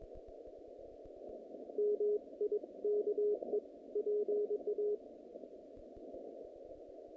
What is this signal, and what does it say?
Morse code, heard as a keyed tone, about 22 wpm, 420 Hz. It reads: MIKE PA
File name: contests / naqp-cw-k5tr-op-k5pi- -2200